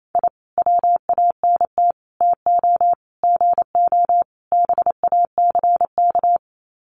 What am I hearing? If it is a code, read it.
Morse code, heard as a keyed tone, about 28 wpm, 700 Hz. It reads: I WANT TO GO BACK
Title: Sets of 5 Words Encore 28wpm